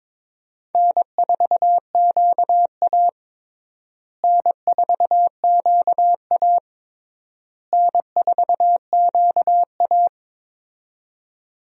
Morse code, heard as a keyed tone, about 22 wpm, 700 Hz. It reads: N4QA N4QA N4QA
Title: USA Call Signs - 4 Characters - Repeated 3x - 22wpm